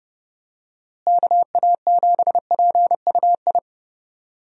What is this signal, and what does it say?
Morse code, heard as a keyed tone, about 30 wpm, 700 Hz. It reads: KA7PUI
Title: USA Call Signs - 6 Characters - Repeated 3x - 30wpm